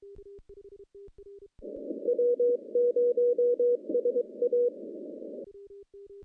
Morse code, heard as a keyed tone, about 23 wpm, 480 Hz. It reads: W0SA
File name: contests / CW-SS-k5tr- -0200